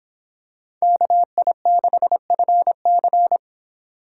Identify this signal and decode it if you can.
Morse code, heard as a keyed tone, about 25 wpm, 700 Hz. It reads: KI6FC